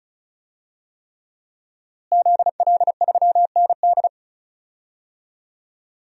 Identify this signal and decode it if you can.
Morse code, heard as a keyed tone, about 34 wpm, 700 Hz. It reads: ZL3ND
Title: DXCC-100 Call Signs - 5 Characters - Repeated 3x - 35wpm